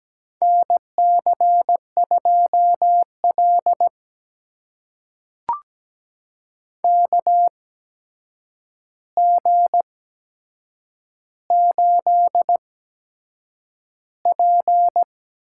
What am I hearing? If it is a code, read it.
Morse code, heard as a keyed tone, about 17 wpm, spaced as for 7 wpm, 700 Hz. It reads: NC2L KG8P
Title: USA Call Signs - 4 Characters - 8x Character Spacing - 17wpm